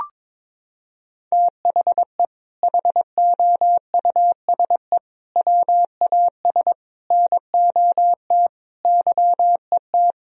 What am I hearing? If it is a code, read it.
Morse code, heard as a keyed tone, about 22 wpm, 700 Hz. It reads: THE HOUSE WAS NOT YET
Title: Sentences from Top 1000 Words - Review Part 2 - 22wpm